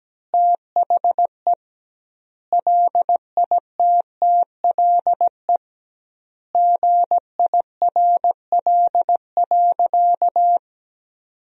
Morse code, heard as a keyed tone, about 17 wpm, 700 Hz. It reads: THE LITTLE GIRL.
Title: 2x Word-Spacing - Sentences from Top 400 Words - 17wpm